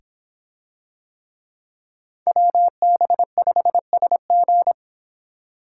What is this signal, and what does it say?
Morse code, heard as a keyed tone, about 26 wpm, 700 Hz. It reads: WB5SG